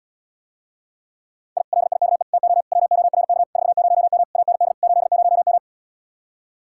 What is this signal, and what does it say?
Morse code, heard as a keyed tone, about 77 wpm, 700 Hz. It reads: I LIKE TO WORK 40M AND 80M